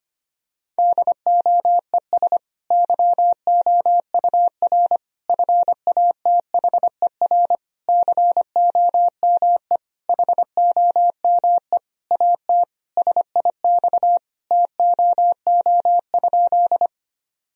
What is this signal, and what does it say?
Morse code, heard as a keyed tone, about 25 wpm, 700 Hz. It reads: DOES YOUR FATHER COME HOME AT SIX TOO?
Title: Sentences from Top 1000 Words - Review Part 2 - 25wpm